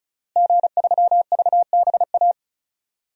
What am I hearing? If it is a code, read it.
Morse code, heard as a keyed tone, about 36 wpm, 700 Hz. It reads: G3VBA
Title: DXCC-100 Call Signs - 5 Characters - Repeated 3x - 35wpm